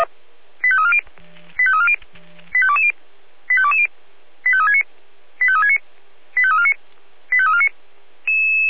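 Five-tone selective call (ZVEI), heard as quick runs of five tones, tones of about 70 ms. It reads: 85340 85340 85220 85220 85369 85369 85349 85349
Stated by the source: ZVEI five-tone call 85276 received by SDR.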